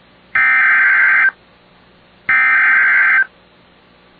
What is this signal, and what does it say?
Emergency Alert System (SAME) tones (header bursts), heard as harsh buzzing bursts. It is heard from 0.4 to 3.2 s.